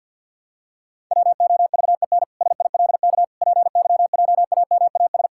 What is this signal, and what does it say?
Morse code, heard as a keyed tone, about 48 wpm, 700 Hz. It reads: WOVEN SILK PYJAMAS
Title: Pangram Challenge v2 - 50wpm